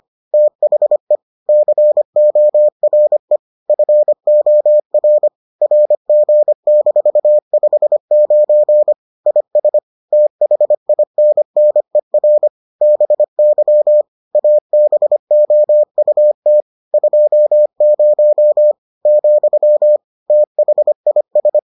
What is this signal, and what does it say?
Morse code, heard as a keyed tone, about 25 wpm, 600 Hz. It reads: THE CORE FOR RG-59 IS THINNER BY ABOUT 20, THIS